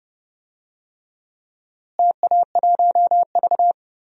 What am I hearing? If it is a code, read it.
Morse code, heard as a keyed tone, about 29 wpm, 700 Hz. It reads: TA1V
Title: DXCC-100 Call Signs - 4 Characters - Repeated 3x - 30wpm